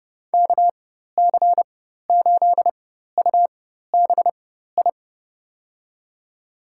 Morse code, heard as a keyed tone, about 30 wpm, spaced as for 14 wpm, 700 Hz. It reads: KC8UBI
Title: USA Call Signs - 6 Characters - 4x Character Spacing - 30wpm